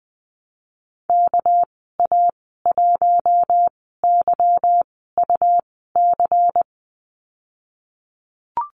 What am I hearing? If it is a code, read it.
Morse code, heard as a keyed tone, about 20 wpm, spaced as for 15 wpm, 700 Hz. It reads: KA1YUC
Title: 207 Lesson 33g - Letter j - Call Signs 20wpm